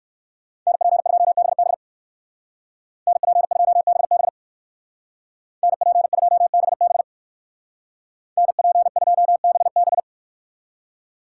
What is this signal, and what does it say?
Morse code, heard as a keyed tone, about 58 wpm, 700 Hz. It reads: NP2BB NP2BB NP2BB NP2BB